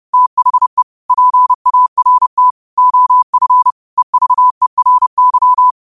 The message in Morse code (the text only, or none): THE PART OF EVERY